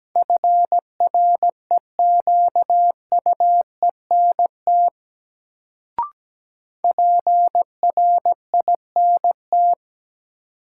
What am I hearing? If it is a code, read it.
Morse code, heard as a keyed tone, about 17 wpm, 700 Hz. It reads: FREQUENT PRINT
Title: Sentences from Top 1400 Words - New Word Drill Alt - 17wpm